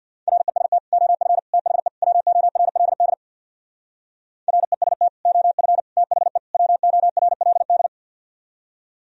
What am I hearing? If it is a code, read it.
Morse code, heard as a keyed tone, about 60 wpm, 700 Hz. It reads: REST OF THE WORLD REST OF THE WORLD
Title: Speed-Racing - Sets of 4 Words 50wpm